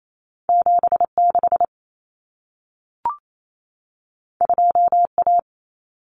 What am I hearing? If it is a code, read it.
Morse code, heard as a keyed tone, about 28 wpm, 700 Hz. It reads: 76 2A
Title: Common Cut Numbers - 2-Digit 28wpm